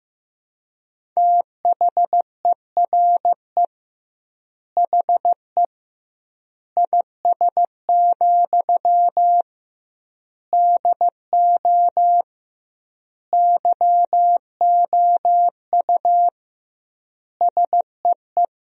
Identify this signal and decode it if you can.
Morse code, heard as a keyed tone, about 15 wpm, 700 Hz. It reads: THERE HE IS, DO YOU SEE
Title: 2x Word-Spacing - Sentences from Top 100 Words - 15wpm